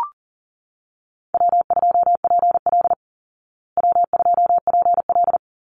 Morse code, heard as a keyed tone, about 40 wpm, 700 Hz. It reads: W2PL W2PL
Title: Easy Call Signs Repeated 2x - 40wpm